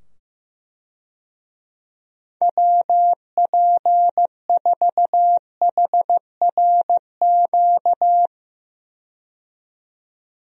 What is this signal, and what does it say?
Morse code, heard as a keyed tone, about 15 wpm, 700 Hz. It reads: WP4HRQ
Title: USA Call Signs - 6 Characters - 15wpm